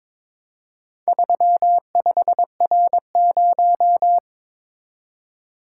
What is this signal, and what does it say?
Morse code, heard as a keyed tone, about 22 wpm, 700 Hz. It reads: 35R0
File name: practice / 22wpm / Four Letter-Number Combinations - 22wpm